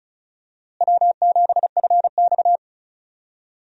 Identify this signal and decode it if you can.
Morse code, heard as a keyed tone, about 34 wpm, 700 Hz. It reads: W7FX
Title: USA Call Signs - 4 Characters - Repeated 3x - 35wpm